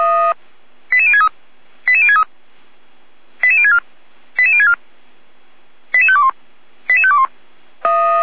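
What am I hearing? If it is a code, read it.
Five-tone selective call (ZVEI), heard as quick runs of five tones, tones of about 80 ms. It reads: 88063 88063 88064 88064 88521 88521
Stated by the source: ZVEI five-tone call received by SDR, logged by Feuerwehr Orlamünde (fire brigade).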